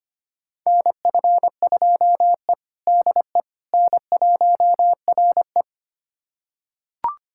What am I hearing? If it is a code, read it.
Morse code, heard as a keyed tone, about 25 wpm, 700 Hz. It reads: NF2E DE N1RE
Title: USA Callsign DE Callsign - 4 Characters - 25wpm